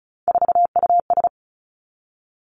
Morse code, heard as a keyed tone, about 36 wpm, 700 Hz. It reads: 4US
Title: Three Challenging Letter-Number Combinations 35wpm